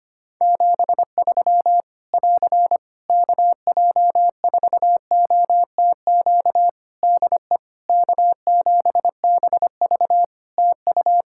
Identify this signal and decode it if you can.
Morse code, heard as a keyed tone, about 25 wpm, 700 Hz. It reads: 73 <AR> KJ4OTQ DE K7BV TU
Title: K1USN SST QSO Practice 25wpm